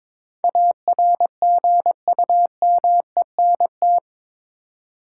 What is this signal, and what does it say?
Morse code, heard as a keyed tone, about 22 wpm, 700 Hz. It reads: ARGUMENT